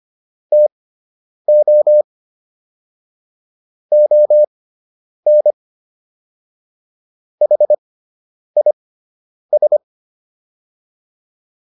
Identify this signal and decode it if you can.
Morse code, heard as a keyed tone, about 25 wpm, spaced as for 9 wpm, 600 Hz. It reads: TO ON HIS